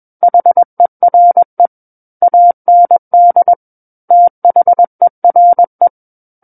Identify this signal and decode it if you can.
Morse code, heard as a keyed tone, about 21 wpm, 700 Hz. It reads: HERE AND THERE
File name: Book One / 12-Chapter 12 - 21 WPM